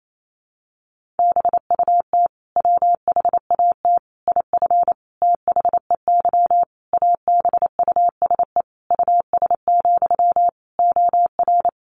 Morse code, heard as a keyed tone, about 28 wpm, 700 Hz. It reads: BUT WHAT IF THEY ABUSE US, OR
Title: Sentences from Top 1400 Words 28wpm